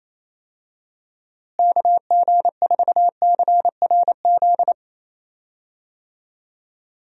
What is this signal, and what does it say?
Morse code, heard as a keyed tone, about 28 wpm, 700 Hz. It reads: KG4CRZ
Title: USA Call Signs - 6 Characters - Repeated 2x - 28wpm